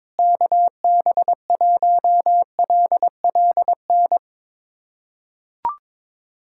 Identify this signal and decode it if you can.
Morse code, heard as a keyed tone, about 22 wpm, 700 Hz. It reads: KB1LLN
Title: USA Callsign DE Callsign - 6 Characters - 22wpm